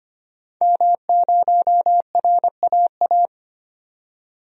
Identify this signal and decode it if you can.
Morse code, heard as a keyed tone, about 25 wpm, 700 Hz. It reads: M0RAA
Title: DXCC-100 Call Signs - 5 Characters - 25wpm